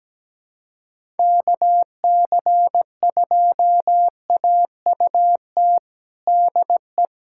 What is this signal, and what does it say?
Morse code, heard as a keyed tone, about 17 wpm, 700 Hz. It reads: KC2AUT DE
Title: USA Callsign DE Callsign - 6 Characters - 17wpm